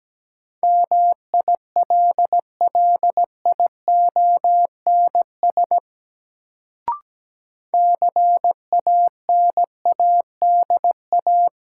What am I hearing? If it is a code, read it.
Morse code, heard as a keyed tone, about 17 wpm, 700 Hz. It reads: MILLIONS CANADA
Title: Sets of 5 Words - New Word Drill - 17wpm